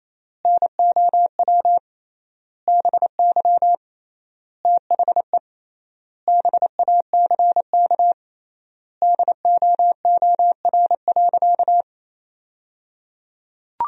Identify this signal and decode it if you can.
Morse code, heard as a keyed tone, about 28 wpm, 700 Hz. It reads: NOW BY THE BACK DOOR.
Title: 3x Word-Spacing - Sentences from Top 300 Words - 28wpm